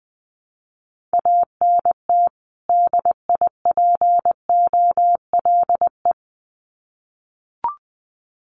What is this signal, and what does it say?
Morse code, heard as a keyed tone, about 20 wpm, 700 Hz. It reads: ANT DIPOLE
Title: Common QSO Phrases 20wpm